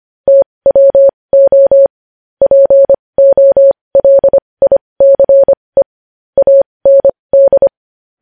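Morse code, heard as a keyed tone, about 25 wpm, spaced as for 20 wpm, 550 Hz. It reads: TWO POLICE AND
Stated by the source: Morse code news headlines practice at 20 wpm.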